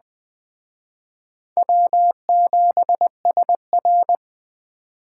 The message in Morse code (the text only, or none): W7SR